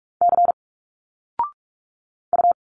Morse code, heard as a keyed tone, about 46 wpm, 700 Hz. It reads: C U
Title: Single Letters 45wpm